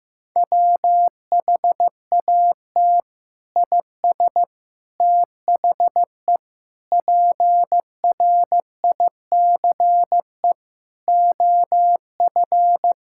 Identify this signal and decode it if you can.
Morse code, heard as a keyed tone, about 15 wpm, 700 Hz. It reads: WHAT IS THE PRICE OF
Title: Sentences from Top 1000 Words - Review Part 5 - 15wpm